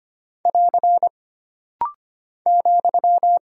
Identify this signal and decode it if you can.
Morse code, heard as a keyed tone, about 25 wpm, 700 Hz. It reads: <AR> ,
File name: Single Punctuation-Prosign - Rapid-Fire - 25wpm